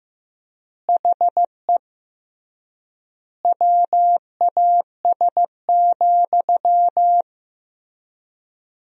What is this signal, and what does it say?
Morse code, heard as a keyed tone, about 15 wpm, 700 Hz. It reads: HE WAS,
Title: 3x Word-Spacing - Sentences from Top 400 Words - 15wpm